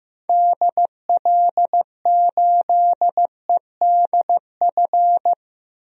Morse code, heard as a keyed tone, about 15 wpm, 700 Hz. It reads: DL8EDF
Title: DXCC-100 Call Signs - 6 Characters - Repeated 3x - 15wpm